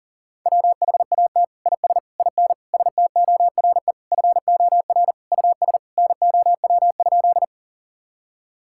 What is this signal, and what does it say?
Morse code, heard as a keyed tone, about 40 wpm, 700 Hz. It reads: WHAT IS IN STORE FOR US NOW?
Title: Sentences from Top 1000 Words - Review Part 5 - 40wpm